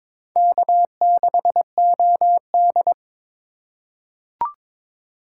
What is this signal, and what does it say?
Morse code, heard as a keyed tone, about 22 wpm, 700 Hz. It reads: K6OD
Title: USA Call Signs - 4x Character Spacing - 22wpm